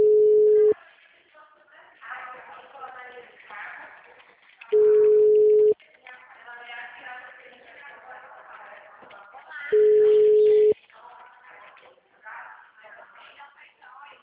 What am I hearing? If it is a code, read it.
Ringback tone (Europe, 425 Hz), with speech.